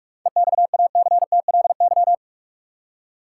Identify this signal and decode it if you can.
Morse code, heard as a keyed tone, about 46 wpm, 700 Hz. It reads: EXACTLY
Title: Seven Letter Words - 45wpm